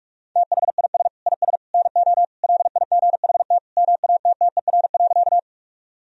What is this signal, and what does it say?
Morse code, heard as a keyed tone, about 46 wpm, 700 Hz. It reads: THIS IS NO LIGHT MATTER.
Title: Sentences from Top 1000 Words - Review Part 1 - 45wpm